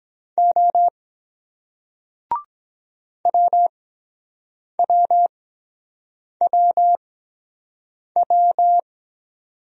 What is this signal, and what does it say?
Morse code, heard as a keyed tone, about 23 wpm, spaced as for 7 wpm, 700 Hz. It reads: O WWWW